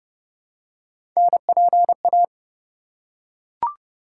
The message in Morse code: NPA

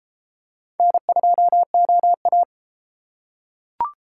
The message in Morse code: N2OA

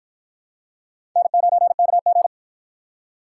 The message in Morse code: N9KG

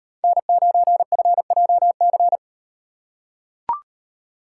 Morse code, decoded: N9FJC

